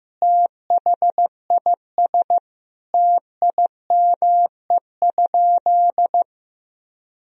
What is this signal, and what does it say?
Morse code, heard as a keyed tone, about 16 wpm, 700 Hz. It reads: THIS TIME?
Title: Sentences from Top 1000 Words - Review Part 2 - 15wpm